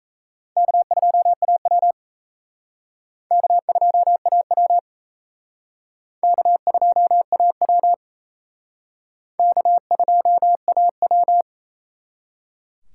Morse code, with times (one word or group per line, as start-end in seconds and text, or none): 0.6-1.9 K2AW
3.3-4.8 K2AW
6.2-7.9 K2AW
9.4-11.4 K2AW